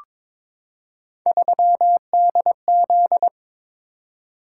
Morse code, heard as a keyed tone, about 22 wpm, 700 Hz. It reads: 3DZ
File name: practice / 22wpm / Three Letter-Number Combinations - 22wpm